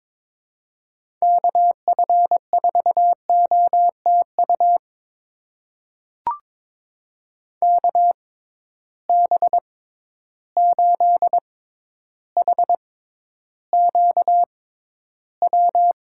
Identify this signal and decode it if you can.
Morse code, heard as a keyed tone, about 22 wpm, spaced as for 11 wpm, 700 Hz. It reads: KF4OTU KB8HQW
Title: USA Call Signs - 6 Characters - 6x Character Spacing - 22wpm